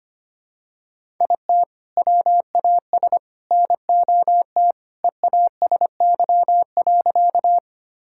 Morse code, heard as a keyed tone, about 25 wpm, 700 Hz. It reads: IT WAS NOT EASY.
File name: practice / 25wpm / Sentences with 3-Word Phrases - 25wpm